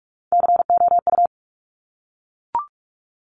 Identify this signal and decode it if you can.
Morse code, heard as a keyed tone, about 46 wpm, 700 Hz. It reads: COU
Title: Top Three Letter Combinations in Words 45wpm